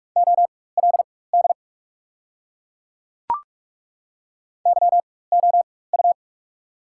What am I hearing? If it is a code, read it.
Morse code, heard as a keyed tone, about 42 wpm, spaced as for 20 wpm, 690 Hz. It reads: OLD YOU
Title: 4x Character Spacing - Top 200 Words - 45wpm